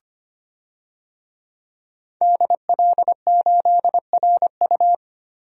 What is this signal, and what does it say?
Morse code, heard as a keyed tone, about 25 wpm, 700 Hz. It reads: DL8RU
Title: DXCC-100 Call Signs - 5 Characters - Repeated 2x - 25wpm